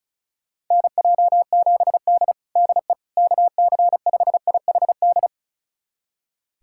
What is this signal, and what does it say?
Morse code, heard as a keyed tone, about 34 wpm, 700 Hz. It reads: NJ7D DE KC5IHD
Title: USA Callsign DE Callsign 35wpm